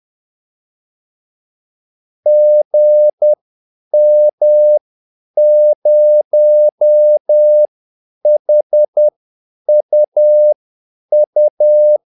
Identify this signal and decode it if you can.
Morse code, heard as a keyed tone, about 10 wpm, spaced as for 8 wpm, 600 Hz. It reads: GM0HUU
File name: Callsigns-10-8wpm